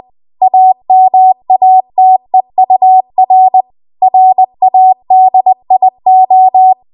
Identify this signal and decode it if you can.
Morse code, heard as a keyed tone, about 20 wpm, 750 Hz. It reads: AMATEUR RADIO
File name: Archive / 20 WPM / 20WPM